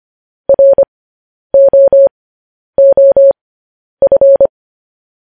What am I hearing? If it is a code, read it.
Morse code, heard as a keyed tone, about 25 wpm, spaced as for 10 wpm, 550 Hz. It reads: ROOF
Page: news headlines in morse code at 10 wpm